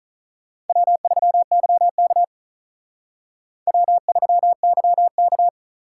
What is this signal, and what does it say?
Morse code, heard as a keyed tone, about 44 wpm, 700 Hz. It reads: W3YK W3YK